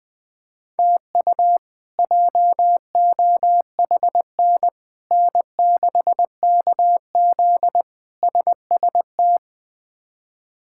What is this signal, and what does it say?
Morse code, heard as a keyed tone, about 20 wpm, 700 Hz. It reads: TU JOHN N6KZ SST
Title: K1USN SST QSO Practice - 20wpm